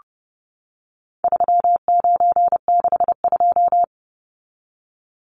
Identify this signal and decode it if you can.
Morse code, heard as a keyed tone, about 30 wpm, 700 Hz. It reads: 3962
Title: Numbers - 4-Digit 30wpm